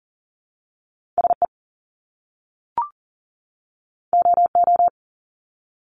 Morse code, heard as a keyed tone, about 40 wpm, 700 Hz. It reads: SE OO
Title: Top Two Letter Combinations in Words 40wpm